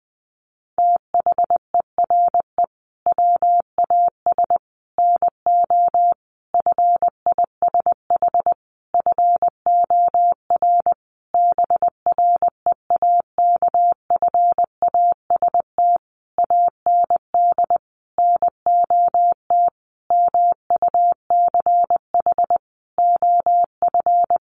Morse code, heard as a keyed tone, about 20 wpm, 700 Hz. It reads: THERE WAS NO FISH FOR BREAKFAST AND NOT MUCH OF
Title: Sentences from Top 1500 Words 20wpm